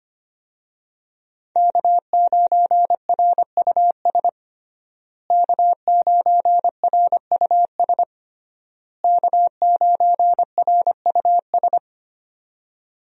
Morse code, heard as a keyed tone, about 25 wpm, 700 Hz. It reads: K9RUS K9RUS K9RUS
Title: USA Call Signs - 5 Characters - Repeated 3x - 25wpm